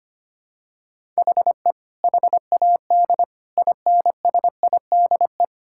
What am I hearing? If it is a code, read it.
Morse code, heard as a keyed tone, about 25 wpm, 700 Hz. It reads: HE HAD INSIDE
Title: Sentences from Top 1000 Words - Review Part 6 - 25wpm